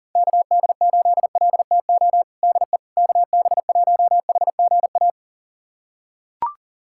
Morse code, heard as a keyed tone, about 40 wpm, 700 Hz. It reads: KD8LTO DE KB1HGA
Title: USA Callsign DE Callsign - 6 Characters - 40wpm